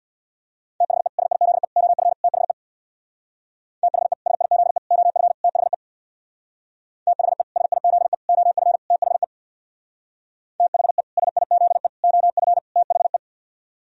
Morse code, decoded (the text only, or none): THE SIZE OF THE THE SIZE OF THE THE SIZE OF THE THE SIZE OF THE